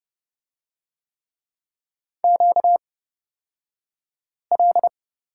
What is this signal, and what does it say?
Morse code, heard as a keyed tone, about 30 wpm, spaced as for 5 wpm, 700 Hz. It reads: QL
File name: Lesson 38d - Letter q - Two Characters 30wpm